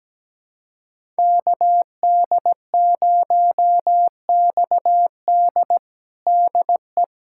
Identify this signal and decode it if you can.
Morse code, heard as a keyed tone, about 17 wpm, 700 Hz. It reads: KD0XD DE
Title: USA Callsign DE Callsign - 5 Characters - 17wpm